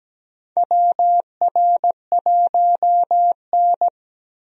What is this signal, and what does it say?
Morse code, heard as a keyed tone, about 17 wpm, 700 Hz. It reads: WR1N
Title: USA Call Signs - 4 Characters - 17wpm